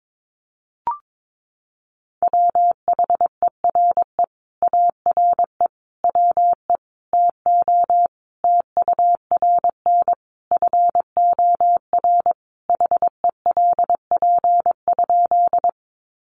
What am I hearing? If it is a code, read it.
Morse code, heard as a keyed tone, about 22 wpm, 700 Hz. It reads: WHERE ARE WE TO TURN FOR HELP?
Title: Sentences from Top 200 Words 22wpm